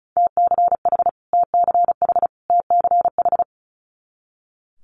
Morse code, heard as a keyed tone, about 36 wpm, 700 Hz. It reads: TCH TCH TCH
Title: Multi-letter Phonemes Repeated 3x 35wpm